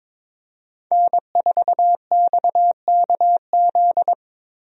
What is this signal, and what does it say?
Morse code, heard as a keyed tone, about 21 wpm, 700 Hz. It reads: N4XKZ
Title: USA Call Signs - 5 Characters - Repeated 2x - 22wpm